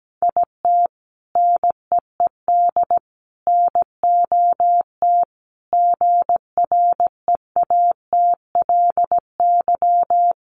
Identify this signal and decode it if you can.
Morse code, heard as a keyed tone, about 17 wpm, 700 Hz. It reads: IT NEED NOT GREATLY
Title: Sentences from Top 1400 Words 17wpm